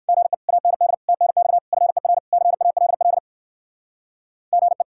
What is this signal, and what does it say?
Morse code, heard as a keyed tone, about 50 wpm, 700 Hz. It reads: GE AND TNX FER CALL GE
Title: Speed-Racing Common QSO Phrases Encore - 45wpm